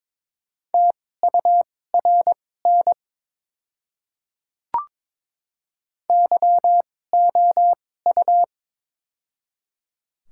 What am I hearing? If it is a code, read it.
Morse code, heard as a keyed tone, about 21 wpm, spaced as for 16 wpm, 700 Hz. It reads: TURN YOU
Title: 2x Character Spacing - Top 300 Words - 22wpm